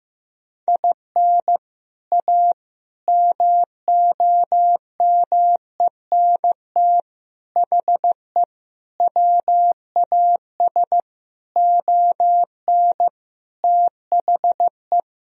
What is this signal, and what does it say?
Morse code, heard as a keyed tone, about 16 wpm, 700 Hz. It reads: IN A MOMENT HE WAS ON THE
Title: Sentences from Top 1000 Words - Review Part 2 - 15wpm